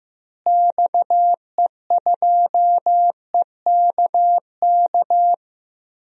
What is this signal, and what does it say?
Morse code, heard as a keyed tone, about 15 wpm, 700 Hz. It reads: XE2EKK
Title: DXCC-100 Call Signs - 6 Characters - Repeated 2x - 15wpm